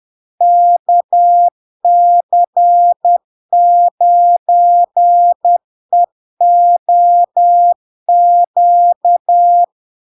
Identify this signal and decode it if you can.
Morse code, heard as a keyed tone, about 10 wpm, 700 Hz. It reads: KC9EOQ
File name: LARK-FD-2022-Log-in-CW-10wpm-700-Hz